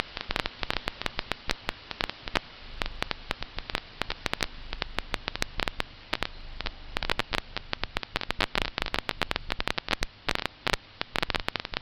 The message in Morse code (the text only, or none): none